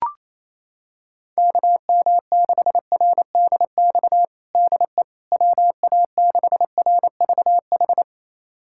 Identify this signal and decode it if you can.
Morse code, heard as a keyed tone, about 28 wpm, 700 Hz. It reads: KM6RDX DE WA6RVH
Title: USA Callsign DE Callsign - 6 Characters - 28wpm